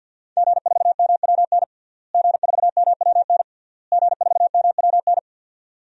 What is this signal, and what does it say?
Morse code, heard as a keyed tone, about 52 wpm, 700 Hz. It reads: G4MWN G4MWN G4MWN
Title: DXCC-100 Call Signs - 5 Characters - Repeated 3x - 50wpm